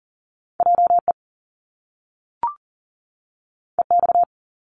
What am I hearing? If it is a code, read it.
Morse code, heard as a keyed tone, about 42 wpm, 700 Hz. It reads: JE EX
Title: Top Two Letter Combinations in Words 40wpm